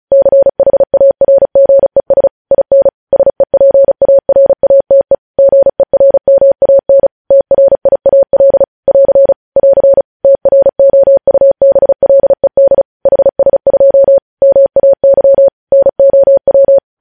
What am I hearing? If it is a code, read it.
Morse code, heard as a keyed tone, about 36 wpm, 550 Hz. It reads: CHARGES IN SEPARATE GERMAN TRIAL <AR> <AR> TROUBLED HS2 MAY NOW